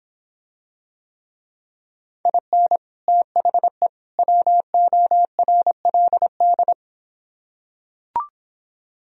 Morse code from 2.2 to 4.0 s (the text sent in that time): IN THE